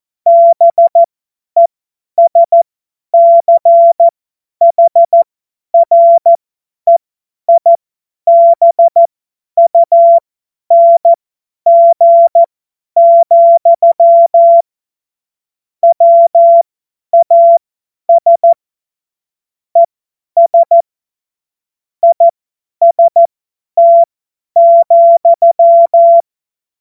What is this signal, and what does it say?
Morse code, heard as a keyed tone, about 14 wpm, spaced as for 10 wpm, 680 Hz. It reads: BESCHREIBUNG, WAS ES IST,
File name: Morsekurs / Witze / Witz 14. Juni 2022 50 F70